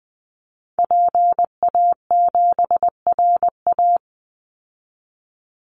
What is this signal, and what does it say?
Morse code, heard as a keyed tone, about 19 wpm, 700 Hz. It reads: PA7RA
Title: Call Signs Repeated 3x - 20wpm